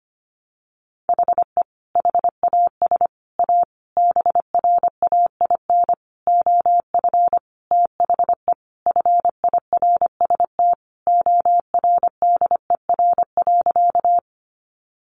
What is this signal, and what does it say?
Morse code, heard as a keyed tone, about 25 wpm, 700 Hz. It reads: HE HAS A BRAIN OF THE FIRST ORDER.
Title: Sentences from Top 1100 Words 25wpm